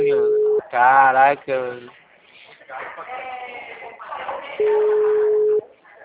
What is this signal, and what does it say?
Ringback tone (Europe, 425 Hz), with speech.